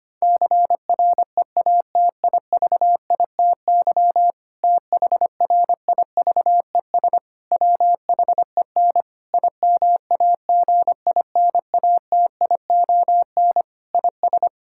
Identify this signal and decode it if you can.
Morse code, heard as a keyed tone, about 25 wpm, 700 Hz. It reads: CREATIVITY THRIVES WHEN IMAGINATION IS
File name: practice / 25wpm / Difficult Sentences with Hints - 25wpm